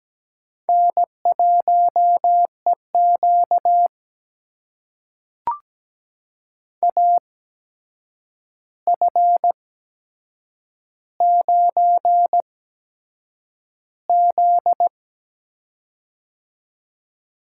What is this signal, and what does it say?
Morse code, heard as a keyed tone, about 17 wpm, spaced as for 7 wpm, 700 Hz. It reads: N1EQ AF9Z
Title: USA Call Signs - 4 Characters - 8x Character Spacing - 17wpm